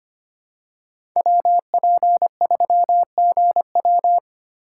Morse code, heard as a keyed tone, about 25 wpm, 700 Hz. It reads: WP3GW
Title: USA Call Signs - 5 Characters - 2x Character Spacing - 25wpm